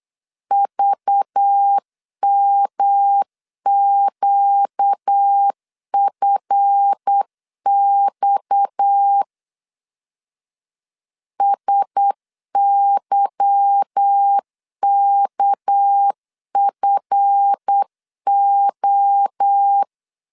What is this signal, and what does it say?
Morse code, heard as a keyed tone, about 8 wpm, 800 Hz. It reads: VMQFX SYKFO
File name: LEZIONI / 8 YJ45 / 8 YJ 8wpm 3313